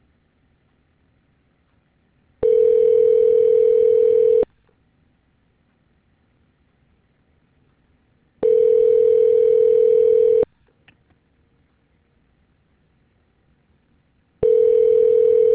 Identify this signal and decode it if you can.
Ringback tone (North America), 440 + 480 Hz.